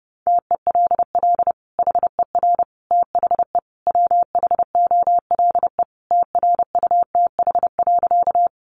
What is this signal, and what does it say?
Morse code, heard as a keyed tone, about 30 wpm, 700 Hz. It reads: TELL HER THE WHOLE TRUTH.
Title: Sentences from Top 800 Words 30wpm